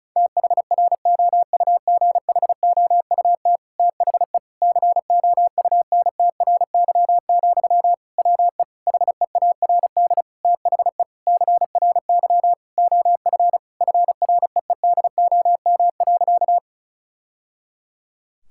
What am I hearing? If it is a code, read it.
Morse code, heard as a keyed tone, about 34 wpm, 700 Hz. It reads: THROUGHOUT THE COUNTRY, WE HEARD THE CRY OF FREEDOM.
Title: Sentences from Top 1000 Words - Review Part 5 - 35wpm